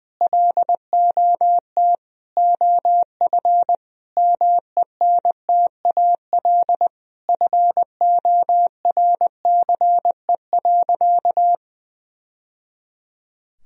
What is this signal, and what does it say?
Morse code, heard as a keyed tone, about 20 wpm, 700 Hz. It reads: LOT OF MENTAL FORCE.
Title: Sentences from Top 1000 Words - Review Part 4 - 20wpm